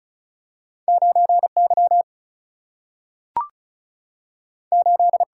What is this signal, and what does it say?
Morse code, heard as a keyed tone, about 36 wpm, 700 Hz. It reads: 9 Y 8
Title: Single-Letter Single-Number 35wpm